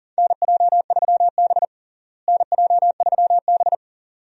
Morse code, heard as a keyed tone, about 40 wpm, 700 Hz. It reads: NJ3B NJ3B
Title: USA Call Signs - 4 Characters - Repeated 3x - 40wpm